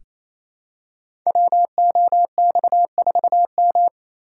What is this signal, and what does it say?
Morse code, heard as a keyed tone, about 28 wpm, 700 Hz. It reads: WOX4M
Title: Code Groups - 5 Letter:Numbers - 28wpm